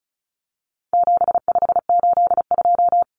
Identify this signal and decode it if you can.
Morse code, heard as a keyed tone, about 34 wpm, 700 Hz. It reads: 7582
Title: Numbers - 4-Digit 35wpm